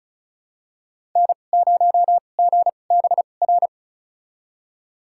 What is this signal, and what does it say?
Morse code, heard as a keyed tone, about 36 wpm, spaced as for 25 wpm, 700 Hz. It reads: N0GBR
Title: USA Call Signs - 2x Character Spacing - 35wpm